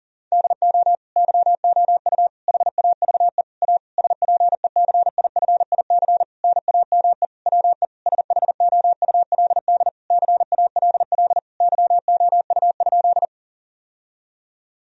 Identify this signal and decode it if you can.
Morse code, heard as a keyed tone, about 39 wpm, 700 Hz. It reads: DO YOU HAVE A SPECIFIC NAME WE SHOULD CALL YOU?
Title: Sentences from Top 1000 Words - Review Part 3 - 40wpm